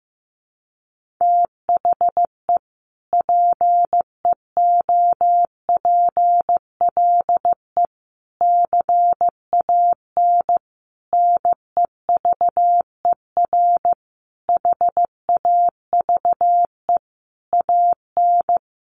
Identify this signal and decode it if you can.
Morse code, heard as a keyed tone, about 15 wpm, 700 Hz. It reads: THE PEOPLE CAN NEVER HAVE AN
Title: Sentences from Top 1400 Words 15wpm